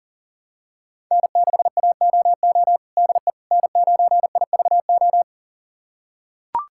Morse code, heard as a keyed tone, about 40 wpm, 700 Hz. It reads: N6AOO DE N9IVO
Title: USA Callsign DE Callsign - 5 Characters - 40wpm